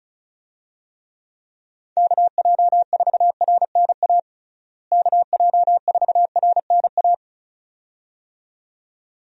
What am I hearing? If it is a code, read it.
Morse code, heard as a keyed tone, about 36 wpm, 700 Hz. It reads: KJ4RNA KJ4RNA
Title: USA Call Signs - 6 Characters - Repeated 2x - 35wpm